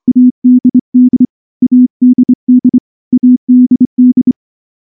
Morse code, heard as a keyed tone, about 24 wpm, 250 Hz. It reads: ADD ADD ADD
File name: morse-training-25WPM